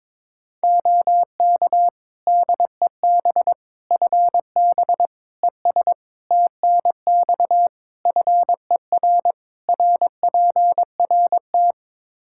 Morse code, heard as a keyed tone, about 22 wpm, 700 Hz. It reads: OK DEB FB ES TNX FER RPRT
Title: Practice QSO - Exchange 2 - Info - 22wpm